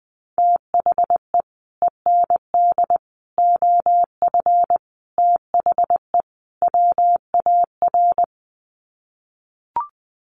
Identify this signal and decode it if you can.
Morse code, heard as a keyed tone, about 20 wpm, 700 Hz. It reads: THE END OF THE WAR
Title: Sets of 5 Words 20wpm